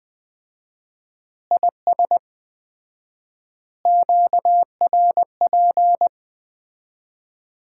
Morse code, heard as a keyed tone, about 20 wpm, 700 Hz. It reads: IS QRP
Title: Common QSO Phrases - 4x Word-Spacing - 20wpm